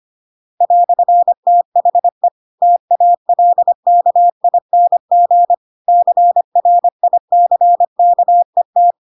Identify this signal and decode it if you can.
Morse code, heard as a keyed tone, about 25 wpm, 700 Hz. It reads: "THE TALKING CRICKET
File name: Pinocchio / Pinocchio ch5 25 25 ik1hge